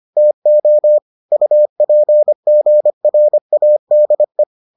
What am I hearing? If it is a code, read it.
Morse code, heard as a keyed tone, about 25 wpm, 600 Hz. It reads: TO UPGRADE